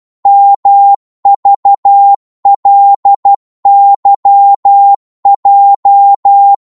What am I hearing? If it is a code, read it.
Morse code, heard as a keyed tone, about 12 wpm, 800 Hz. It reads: MVLYJ
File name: Morse / ltr-12wpm